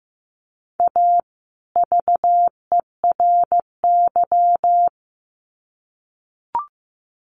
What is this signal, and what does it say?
Morse code, heard as a keyed tone, about 15 wpm, 700 Hz. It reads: A VERY
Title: Sets of 2 Words 15wpm